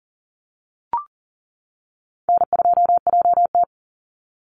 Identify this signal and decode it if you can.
Morse code, heard as a keyed tone, about 40 wpm, 700 Hz. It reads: N2JT
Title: Easy Call Signs - 40wpm